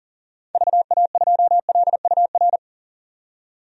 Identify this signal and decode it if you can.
Morse code, heard as a keyed tone, about 40 wpm, 700 Hz. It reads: VA2LUR
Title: DXCC-100 Call Signs - 6 Characters - 40wpm